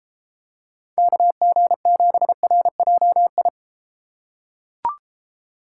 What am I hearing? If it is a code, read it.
Morse code, heard as a keyed tone, about 30 wpm, 700 Hz. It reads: KG7RJI